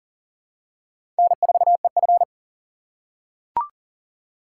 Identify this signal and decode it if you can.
Morse code, heard as a keyed tone, about 40 wpm, 700 Hz. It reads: N4EF
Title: USA Call Signs - 4 Characters - 6x Character Spacing - 40wpm